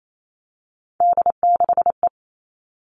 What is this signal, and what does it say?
Morse code, heard as a keyed tone, about 28 wpm, 700 Hz. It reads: D6E
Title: Three Challenging Letter-Number Combinations 28wpm